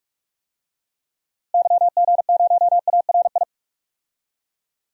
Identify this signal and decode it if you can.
Morse code, heard as a keyed tone, about 46 wpm, 700 Hz. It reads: YG0ARI
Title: DXCC-100 Call Signs - 6 Characters - Repeated 3x - 45wpm